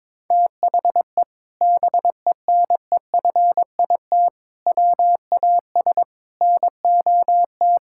Morse code, heard as a keyed tone, about 22 wpm, 700 Hz. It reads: THE BENEFIT WAS NOT
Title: Sentences from Top 1000 Words - Review Part 6 - 22wpm